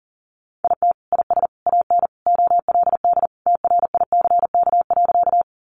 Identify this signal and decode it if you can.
Morse code, heard as a keyed tone, about 40 wpm, 700 Hz. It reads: IT IS AN OLD TRICK.
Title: Sentences from Top 1500 Words 40wpm